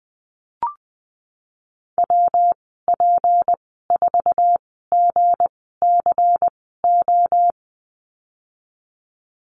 Morse code, heard as a keyed tone, about 20 wpm, spaced as for 15 wpm, 700 Hz. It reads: WP4GCO
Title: 207 Lesson 33g - Letter j - Call Signs 20wpm